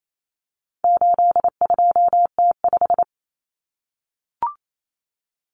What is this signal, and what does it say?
Morse code, heard as a keyed tone, about 28 wpm, 700 Hz. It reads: 82T5
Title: Common Cut Numbers - 4-Digit 28wpm